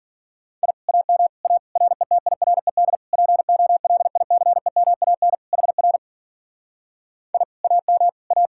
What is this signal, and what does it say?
Morse code, heard as a keyed tone, about 50 wpm, 700 Hz. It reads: I AM A RETIRED POLICEMAN HR I AM A
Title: Speed-Racing Common QSO Phrases - 40wpm